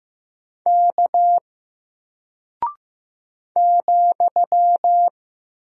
Morse code, heard as a keyed tone, about 15 wpm, 700 Hz. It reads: K ,
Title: Single Character (Letter:Number:Punctuation:Prosign) Alt 15wpm